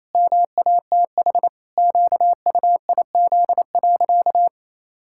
Morse code, heard as a keyed tone, about 28 wpm, 700 Hz. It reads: MATH QUIZ.
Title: Pangram Challenge v2 - 28wpm